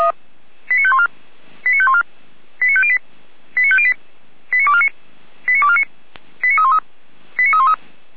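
Five-tone selective call (ZVEI), heard as quick runs of five tones, tones of about 70 ms. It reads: 89514 89514 89508 89508 89249 89249 89313 89313
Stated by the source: ZVEI five-tone selective call received by SDR, on the feuerwehr-orlamuende fire brigade audio server.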